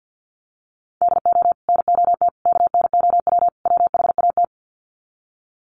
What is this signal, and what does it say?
Morse code, heard as a keyed tone, about 50 wpm, 700 Hz. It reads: DO NOT KNOW WHAT
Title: Sets of 4 Words 50wpm